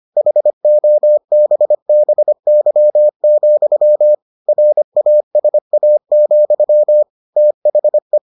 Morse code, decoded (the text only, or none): HOBBY, RASA, THE